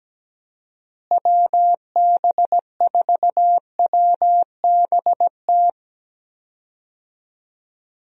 Morse code, heard as a keyed tone, about 17 wpm, 700 Hz. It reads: WB4WBT
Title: USA Call Signs - 6 Characters - Repeated 3x - 17wpm